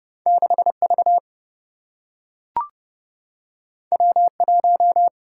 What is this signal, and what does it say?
Morse code, heard as a keyed tone, about 30 wpm, 700 Hz. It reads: 6V W1
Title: Single-Letter Single-Number 30wpm